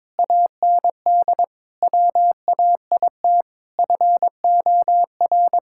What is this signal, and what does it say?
Morse code, heard as a keyed tone, about 22 wpm, 700 Hz. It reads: AND WAIT FOR
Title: Sentences from Top 1600 Words 22wpm